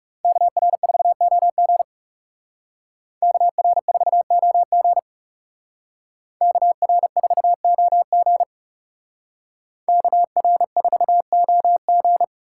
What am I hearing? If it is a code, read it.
Morse code, heard as a keyed tone, about 34 wpm, 700 Hz. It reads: KR4OG KR4OG KR4OG KR4OG
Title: Speed Racing - USA Call Signs 30wpm